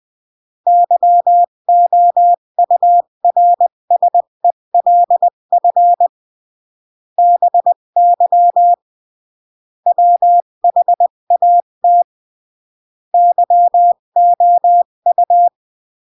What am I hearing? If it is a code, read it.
Morse code, heard as a keyed tone, about 20 wpm, 700 Hz. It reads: YOURSELF BY WHAT YOU
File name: CW / jodorowsky / 82 commandments 20-15